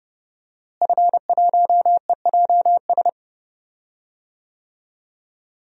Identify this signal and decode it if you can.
Morse code, heard as a keyed tone, about 31 wpm, 700 Hz. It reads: F1EJS
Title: DXCC-100 Call Signs - 5 Characters - Repeated 3x - 30wpm